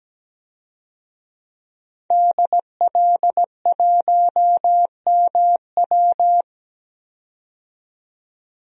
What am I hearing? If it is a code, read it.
Morse code, heard as a keyed tone, about 17 wpm, 700 Hz. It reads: DL1MW
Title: DXCC-100 Call Signs - 5 Characters - 17wpm